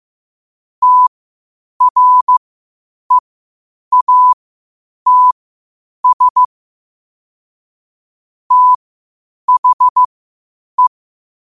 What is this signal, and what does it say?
Morse code, heard as a keyed tone, about 15 wpm, spaced as for 8 wpm, 1000 Hz. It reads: TREATS THE
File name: qotdfiles / 8wpm